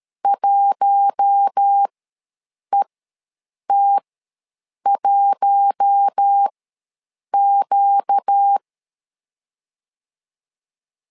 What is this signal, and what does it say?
Morse code, heard as a keyed tone, about 13 wpm, spaced as for 7 wpm, 800 Hz. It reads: 1ET1Q
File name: LEZIONI / 4 Q1 ws 12wpm 2129